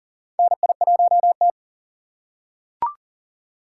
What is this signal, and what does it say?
Morse code, heard as a keyed tone, about 40 wpm, 700 Hz. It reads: NI1T
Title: USA Call Signs - 4 Characters - 6x Character Spacing - 40wpm